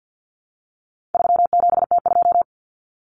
Morse code, heard as a keyed tone, about 50 wpm, 700 Hz. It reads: 37T2
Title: Common Cut Numbers - 4-Digit 50wpm